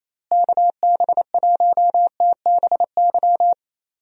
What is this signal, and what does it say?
Morse code, heard as a keyed tone, about 27 wpm, 700 Hz. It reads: KB1TBY
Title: USA Call Signs - 6 Characters - Repeated 3x - 28wpm